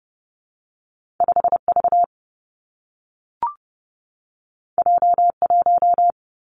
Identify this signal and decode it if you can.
Morse code, heard as a keyed tone, about 30 wpm, 700 Hz. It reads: 5 V J 1
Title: Two Challenging Letter-Number Combinations 30wpm